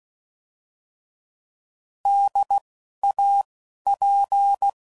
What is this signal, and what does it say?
Morse code, heard as a keyed tone, about 16 wpm, 800 Hz. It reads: D A P
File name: Prefixes / By First Letter / Ds